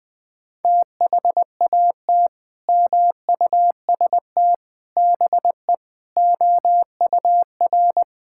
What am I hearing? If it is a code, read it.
Morse code, heard as a keyed tone, about 20 wpm, 700 Hz. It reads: THAT MUST BE OUR
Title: Sentences from Top 1000 Words - Review Part 5 - 20wpm